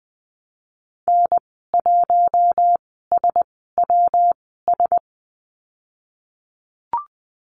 Morse code, heard as a keyed tone, about 19 wpm, spaced as for 14 wpm, 700 Hz. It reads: N1SWS
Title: 122 Lesson 20g - Number 3 - Call Signs 20wpm